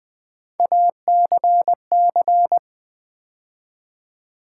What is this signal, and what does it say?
Morse code, heard as a keyed tone, about 20 wpm, 700 Hz. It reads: ACC
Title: Build A Word - 3 Letters At A Time - 6-Letter Words - 20wpm